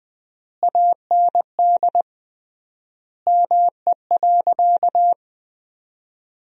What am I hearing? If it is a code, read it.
Morse code, heard as a keyed tone, about 20 wpm, 700 Hz. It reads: AND ME.
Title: 3x Word-Spacing - Sentences from Top 500 Words - 20wpm